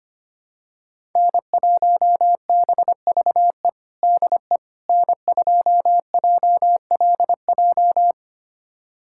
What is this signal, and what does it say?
Morse code, heard as a keyed tone, about 25 wpm, 700 Hz. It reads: N1BVE DE N2JLJ
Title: USA Callsign DE Callsign - 5 Characters - 25wpm